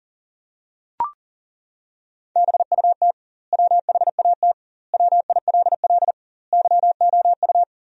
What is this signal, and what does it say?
Morse code, heard as a keyed tone, about 40 wpm, 700 Hz. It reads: BUT WHAT WILL YOU
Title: 2x Word-Spacing - Sentences from Top 100 Words - 40wpm